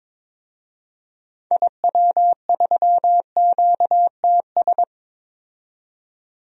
Morse code, heard as a keyed tone, about 22 wpm, 700 Hz. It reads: IW3QTS
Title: DXCC-100 Call Signs - 6 Characters - 22wpm